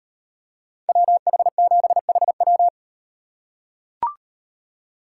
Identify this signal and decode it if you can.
Morse code, heard as a keyed tone, about 38 wpm, 700 Hz. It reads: WH7HW